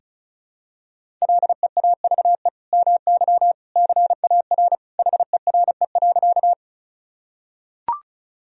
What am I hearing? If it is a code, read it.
Morse code, heard as a keyed tone, about 34 wpm, 700 Hz. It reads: LEAVE MY CAR HERE.
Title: Sentences from Top 1000 Words - Review Part 1 - 35wpm